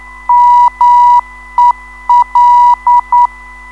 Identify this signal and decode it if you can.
Morse code, heard as a keyed tone, about 9 wpm, 970 Hz. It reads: MEL
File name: ndb zvok / mel-ron-mel